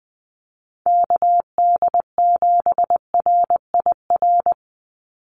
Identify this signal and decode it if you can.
Morse code, heard as a keyed tone, about 20 wpm, 700 Hz. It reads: KD7RIR
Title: USA Call Signs 20wpm